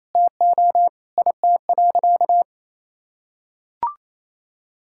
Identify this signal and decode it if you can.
Morse code, heard as a keyed tone, about 28 wpm, 700 Hz. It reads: TO IT.
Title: Sentences from Top 1000 Words - Review Part 5 - 28wpm